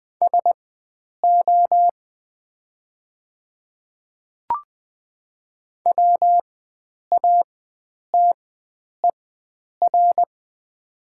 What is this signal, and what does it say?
Morse code, heard as a keyed tone, about 20 wpm, spaced as for 9 wpm, 700 Hz. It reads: SO WATER
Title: 4x Character Spacing - Top 100 Words - 20wpm